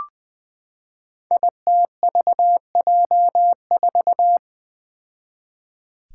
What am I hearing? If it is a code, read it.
Morse code, heard as a keyed tone, about 20 wpm, 700 Hz. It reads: ITVJ4
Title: Five Challenging Letter-Number Combinations 20wpm